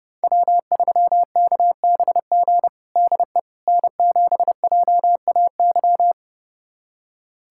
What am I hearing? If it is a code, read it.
Morse code, heard as a keyed tone, about 30 wpm, 700 Hz. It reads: W3KBG DE N7JAY
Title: USA Callsign DE Callsign - 5 Characters - 30wpm